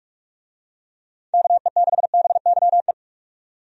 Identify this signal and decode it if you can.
Morse code, heard as a keyed tone, about 42 wpm, 700 Hz. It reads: KE6BYE
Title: USA Call Signs - 6 Characters - 45wpm